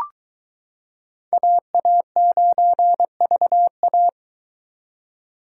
Morse code, heard as a keyed tone, about 22 wpm, 700 Hz. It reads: AA9VA